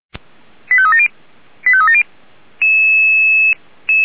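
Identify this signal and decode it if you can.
Five-tone selective call (ZVEI), heard as quick runs of five tones, tones of about 70 ms. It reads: 85370 85370
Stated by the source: ZVEI five-tone selective call 89262 received by SDR, archived by feuerwehr-orlamuende (fire brigade Orlamünde).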